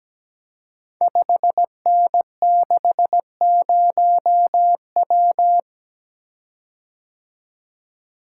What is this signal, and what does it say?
Morse code, heard as a keyed tone, about 17 wpm, 700 Hz. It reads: 5N60W
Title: Code Groups - 5 Letter:Numbers - 17wpm